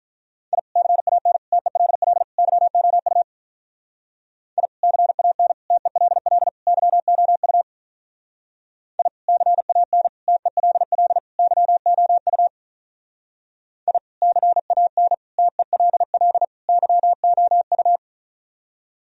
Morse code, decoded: I CAN TELL YOU I CAN TELL YOU I CAN TELL YOU I CAN TELL YOU

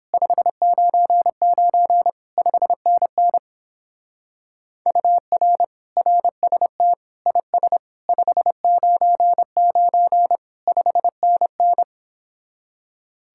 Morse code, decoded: 599 5NN UR RST IS 599 5NN

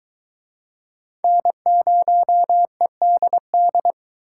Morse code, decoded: N0EDD

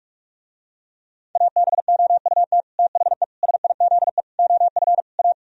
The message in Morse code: ABOUT THE SIZE OF A

